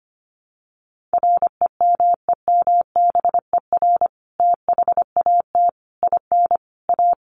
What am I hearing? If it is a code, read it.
Morse code, heard as a keyed tone, about 25 wpm, 700 Hz. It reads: REMEMBER THAT IN A
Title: Sentences from Top 1300 Words 25wpm